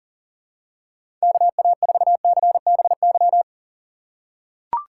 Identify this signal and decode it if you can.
Morse code, heard as a keyed tone, about 40 wpm, 700 Hz. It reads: KA4CBY
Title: USA Call Signs - 6x Character Spacing - 40wpm